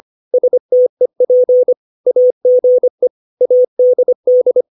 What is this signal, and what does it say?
Morse code, heard as a keyed tone, about 25 wpm, 500 Hz. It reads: STEP AGE ADD
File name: 632 words / 632 Words 25WPM